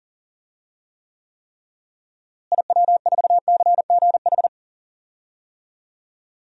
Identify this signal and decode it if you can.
Morse code, heard as a keyed tone, about 37 wpm, 700 Hz. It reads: IW4CGH
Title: DXCC-100 Call Signs - 6 Characters - Repeated 3x - 40wpm